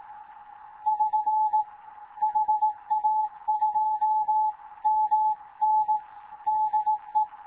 Morse code, heard as a keyed tone, about 18 wpm, 860 Hz. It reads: <SN> HA2MN DE
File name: Morze-HA2MN / hangp / 2 11 GYAKORLAT